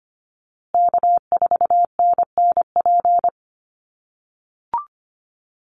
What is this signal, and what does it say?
Morse code, heard as a keyed tone, about 25 wpm, 700 Hz. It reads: K4NNP
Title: USA Call Signs Repeated 3x 25wpm